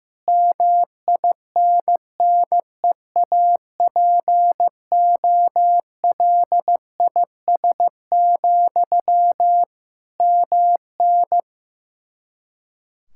Morse code, heard as a keyed tone, about 16 wpm, 700 Hz. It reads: MINNEAPOLIS, MN
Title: Practice QSO - Exchange 1 - Beginning - 15wpm